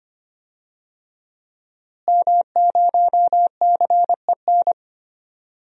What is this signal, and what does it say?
Morse code, heard as a keyed tone, about 25 wpm, 700 Hz. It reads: M0CEN
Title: DXCC-100 Call Signs - 5 Characters - Repeated 2x - 25wpm